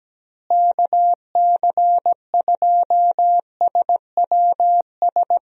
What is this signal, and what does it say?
Morse code, heard as a keyed tone, about 17 wpm, 700 Hz. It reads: KC2SWS
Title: USA Call Signs - 6 Characters - Repeated 3x - 17wpm